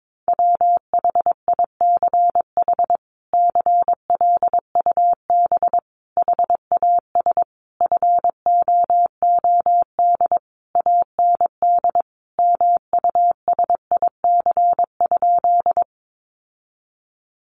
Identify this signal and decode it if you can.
Morse code, heard as a keyed tone, about 22 wpm, 700 Hz. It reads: WHICH CLUB HAS FOOD AND MUSIC?
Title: Sentences from Top 1200 Words 22wpm